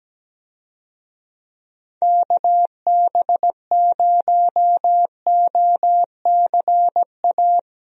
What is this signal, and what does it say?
Morse code, heard as a keyed tone, about 17 wpm, 700 Hz. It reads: KB0OCA
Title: USA Call Signs - 6 Characters - Repeated 2x - 17wpm